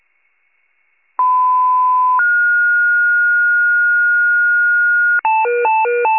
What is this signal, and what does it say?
Two-tone page, heard as a one-second tone, then a three-second tone, about 1010 then 1430 Hz. It is heard from 1.2 to 5.2 s.
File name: wcalert